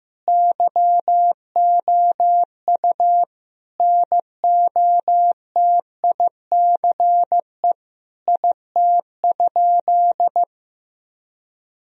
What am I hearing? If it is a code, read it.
Morse code, heard as a keyed tone, about 15 wpm, 700 Hz. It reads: YOU NOTICE IT?
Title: Sentences from Top 1000 Words - Review Part 2 - 15wpm